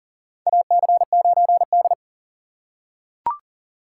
Morse code, heard as a keyed tone, about 40 wpm, 700 Hz. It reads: AC9D
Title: USA Call Signs - 4x Character Spacing - 40wpm